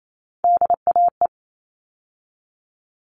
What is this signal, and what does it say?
Morse code, heard as a keyed tone, about 28 wpm, 700 Hz. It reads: DAE
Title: All Cut Numbers - 3-Digit 28wpm